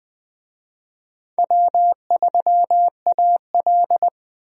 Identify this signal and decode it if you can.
Morse code, heard as a keyed tone, about 20 wpm, 700 Hz. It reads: W3AL
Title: USA Call Signs - 4 Characters - 20wpm